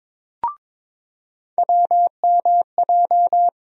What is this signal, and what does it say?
Morse code, heard as a keyed tone, about 22 wpm, 700 Hz. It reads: WMJ
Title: USA Call Sign Postfixes - 3 Letters 22wpm